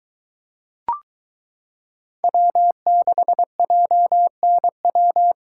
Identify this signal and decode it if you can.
Morse code, heard as a keyed tone, about 22 wpm, 700 Hz. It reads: W6JNW